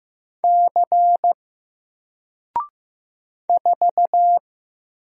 Morse code, heard as a keyed tone, about 15 wpm, 700 Hz. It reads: C 4
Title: Single Character (Letter:Number:Punctuation:Prosign) Alt 15wpm